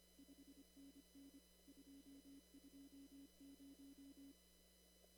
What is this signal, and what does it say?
Morse code, heard as a keyed tone, about 25 wpm, 280 Hz. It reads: 5NN 220
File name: CW / kt5j wpx cw-op-k5pi